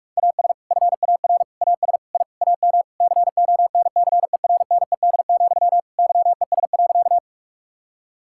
Morse code, heard as a keyed tone, about 46 wpm, 700 Hz. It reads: AS FAR AS I AM CONCERNED, YES.
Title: Sentences from Top 1000 Words - Review Part 6 - 45wpm